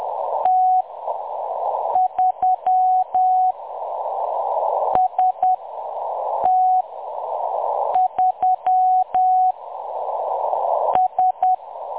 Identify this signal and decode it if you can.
Morse code, heard as a keyed tone, about 10 wpm, spaced as for 6 wpm, 740 Hz. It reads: T3 ST3 S
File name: M23-ST3- -0858utc-10184khz